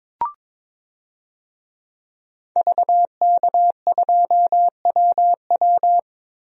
Morse code, heard as a keyed tone, about 22 wpm, 700 Hz. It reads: VK2WW
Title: DXCC-100 Call Signs - 5 Characters - 22wpm